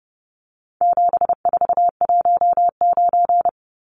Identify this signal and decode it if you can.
Morse code, heard as a keyed tone, about 30 wpm, 700 Hz. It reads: 7419
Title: Numbers - 4-Digit 30wpm